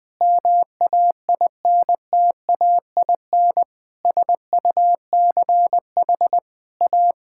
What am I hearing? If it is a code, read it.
Morse code, heard as a keyed tone, about 20 wpm, 700 Hz. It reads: MAINTAIN SUCH A
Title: Sentences from Top 1000 Words - Review Part 5 - 20wpm